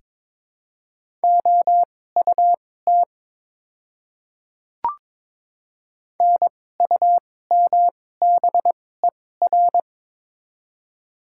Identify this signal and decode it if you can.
Morse code, heard as a keyed tone, about 23 wpm, spaced as for 16 wpm, 700 Hz. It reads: OUT NUMBER
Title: 2x Character Spacing - Top 100 Words - 22wpm